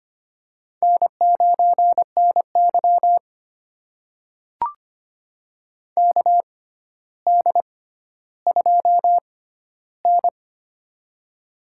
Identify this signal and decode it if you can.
Morse code, heard as a keyed tone, about 25 wpm, spaced as for 13 wpm, 700 Hz. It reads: N9NY KD2N
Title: USA Call Signs - 4 Characters - 6x Character Spacing - 25wpm